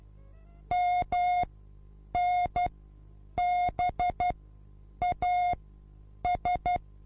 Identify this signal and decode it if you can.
Morse code, heard as a keyed tone, about 12 wpm, spaced as for 8 wpm, 720 Hz. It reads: MNBAS